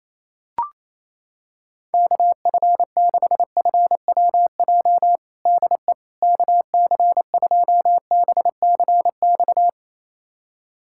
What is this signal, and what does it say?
Morse code, heard as a keyed tone, about 28 wpm, 700 Hz. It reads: KF6FWJ DE KC2BCX
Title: USA Callsign DE Callsign - 6 Characters - 28wpm